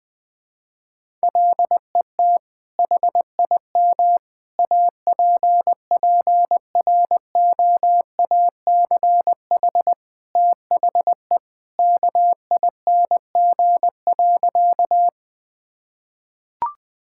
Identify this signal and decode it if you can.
Morse code, heard as a keyed tone, about 21 wpm, 700 Hz. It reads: LET HIM APPROACH THE KING.
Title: Sentences from Top 1000 Words - Review Part 6 - 20wpm